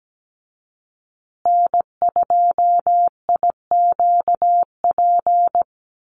Morse code, heard as a keyed tone, about 17 wpm, 700 Hz. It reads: N2IQP
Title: USA Call Signs Repeated 3x 17wpm